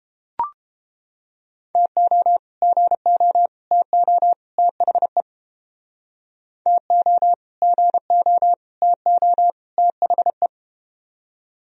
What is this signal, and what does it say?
Morse code, heard as a keyed tone, about 32 wpm, 700 Hz. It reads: TO GO TO THE TO GO TO THE